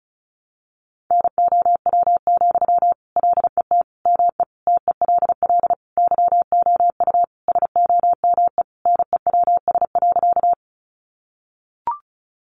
Morse code, heard as a keyed tone, about 34 wpm, 700 Hz. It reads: NOW, LET ME TELL YOU SOME NEWS.
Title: Sentences from Top 800 Words 35wpm